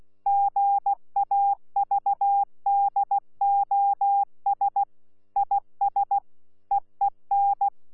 Morse code, heard as a keyed tone, about 16 wpm, 800 Hz. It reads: GAVDOS IS EEN
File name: Z PI4VRZ / PI4VRZ 14-jun-2025